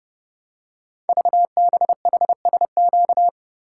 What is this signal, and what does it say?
Morse code, heard as a keyed tone, about 30 wpm, 700 Hz. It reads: VBHSQ
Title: Code Groups - 5 Letter:Numbers - 30wpm